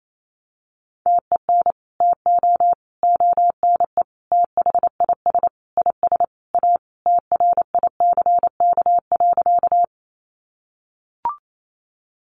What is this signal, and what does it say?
Morse code, heard as a keyed tone, about 28 wpm, 700 Hz. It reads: TEN TO ONE THIS IS A TRICK.
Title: Sentences from Top 1500 Words 28wpm